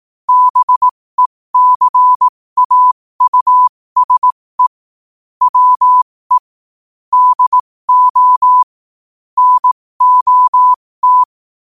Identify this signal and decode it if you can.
Morse code, heard as a keyed tone, about 18 wpm, spaced as for 15 wpm, 1000 Hz. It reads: BECAUSE WE DO NOT
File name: qotdfiles / 15wpm